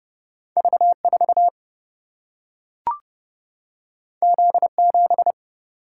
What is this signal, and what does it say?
Morse code, heard as a keyed tone, about 30 wpm, 700 Hz. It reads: V 4 Z 7
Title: Single-Letter Single-Number 30wpm